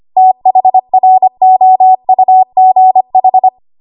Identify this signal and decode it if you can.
Morse code, heard as a keyed tone, about 25 wpm, 750 Hz. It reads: THROUGH
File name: Archive / 25 WPM / 25WPM